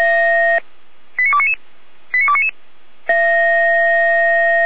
Five-tone selective call (ZVEI), heard as quick runs of five tones, tones of about 70 ms. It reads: 89299 89299
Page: ZVEI